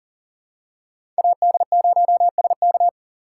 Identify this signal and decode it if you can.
Morse code, heard as a keyed tone, about 37 wpm, 700 Hz. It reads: AD0SK
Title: USA Call Signs - 5 Characters - 2x Character Spacing - 40wpm